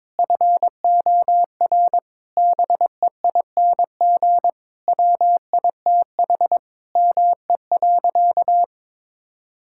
Morse code, heard as a keyed tone, about 22 wpm, 700 Hz. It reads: FOR BEING WITH ME.
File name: Sentences with 9-Word Phrases - 22wpm